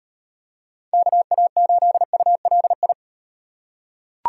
Morse code, heard as a keyed tone, about 38 wpm, 700 Hz. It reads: KA8ULI